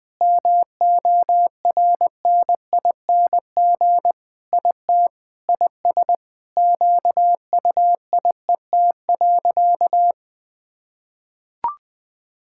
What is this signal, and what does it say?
Morse code, heard as a keyed tone, about 21 wpm, 700 Hz. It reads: MORNING IT IS QUIET.
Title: Sentences with 3-Word Phrases - 20wpm